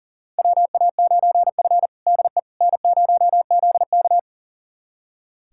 Morse code, heard as a keyed tone, about 40 wpm, 700 Hz. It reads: WA9F DE N0ZK
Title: USA Callsign DE Callsign - 4 Characters - 40wpm